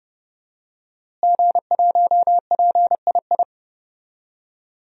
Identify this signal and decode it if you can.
Morse code, heard as a keyed tone, about 30 wpm, 700 Hz. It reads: G1PII
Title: DXCC-100 Call Signs - 5 Characters - Repeated 3x - 30wpm